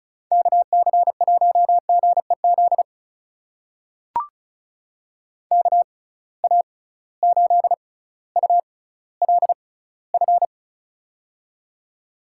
Morse code, heard as a keyed tone, about 36 wpm, spaced as for 18 wpm, 700 Hz. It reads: KC1GEZ KA8ULF
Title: USA Call Signs - 6 Characters - 6x Character Spacing - 35wpm